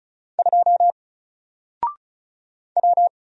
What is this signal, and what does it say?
Morse code, heard as a keyed tone, about 36 wpm, 700 Hz. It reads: 2 W
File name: Single-Letter-Number-35wpm